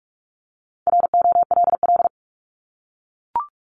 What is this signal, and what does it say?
Morse code, heard as a keyed tone, about 44 wpm, 700 Hz. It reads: ROLL
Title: Top 1751-2000 Words 45wpm